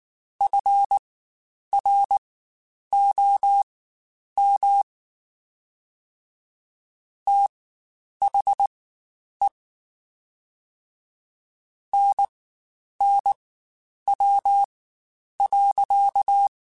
Morse code, heard as a keyed tone, about 19 wpm, spaced as for 9 wpm, 800 Hz. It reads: FROM THE NNW.
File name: Spaced Characters / Spaced Characters-mp3s / Weather5 12 sp @18